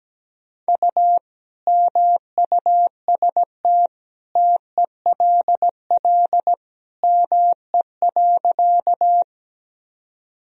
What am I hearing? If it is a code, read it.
Morse code, heard as a keyed tone, about 17 wpm, 700 Hz. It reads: U MUST TELL ME.
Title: Sentences from Top 200 Words and Abbreviations 17wpm